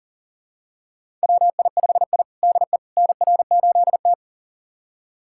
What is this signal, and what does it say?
Morse code, heard as a keyed tone, about 40 wpm, 700 Hz. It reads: WI5I DE NR8T
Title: USA Callsign DE Callsign - 4 Characters - 40wpm